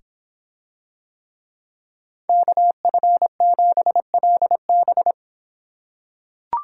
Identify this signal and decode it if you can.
Morse code, heard as a keyed tone, about 26 wpm, 700 Hz. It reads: KF7LB